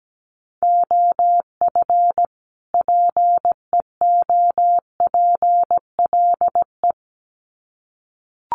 Morse code, heard as a keyed tone, about 17 wpm, 700 Hz. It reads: OF PEOPLE
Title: Sets of 2 Words 17wpm